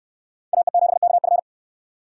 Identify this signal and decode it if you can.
Morse code, heard as a keyed tone, about 64 wpm, 700 Hz. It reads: AE7KF